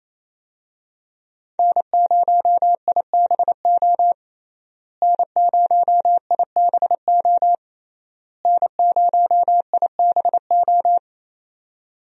Morse code, heard as a keyed tone, about 28 wpm, 700 Hz. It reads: N0IBO N0IBO N0IBO
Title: USA Call Signs - 5 Characters - Repeated 3x - 28wpm